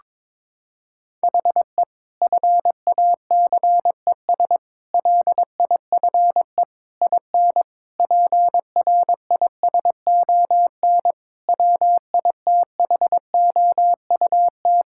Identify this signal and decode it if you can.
Morse code, heard as a keyed tone, about 22 wpm, 700 Hz. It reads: HE FACES LIFE IN PRISON WITHOUT
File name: practice / 22wpm / Sentences with 8-Word Phrases - 22wpm